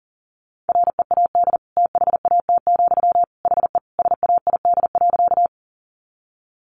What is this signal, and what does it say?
Morse code, heard as a keyed tone, about 38 wpm, 700 Hz. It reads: READ THAT, HE SAID.
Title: Sentences from Top 200 Words 40wpm